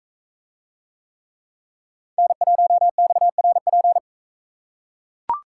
Morse code, heard as a keyed tone, about 41 wpm, 700 Hz. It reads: N1XRP